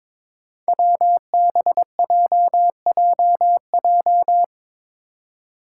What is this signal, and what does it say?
Morse code, heard as a keyed tone, about 22 wpm, 700 Hz. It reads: WBJJJ
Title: Five Challenging Letter-Number Combinations 22wpm